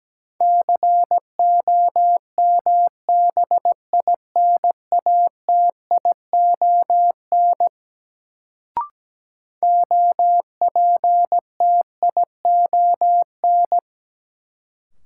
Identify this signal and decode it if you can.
Morse code, heard as a keyed tone, about 17 wpm, 700 Hz. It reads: COMBINATION OPTION
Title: Sentences from Top 1300 Words - New Word Drill Alt - 17wpm